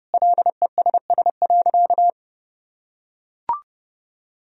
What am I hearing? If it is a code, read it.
Morse code, heard as a keyed tone, about 30 wpm, 700 Hz. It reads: LESS.
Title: Sentences from Top 1000 Words - Review Part 3 - 30wpm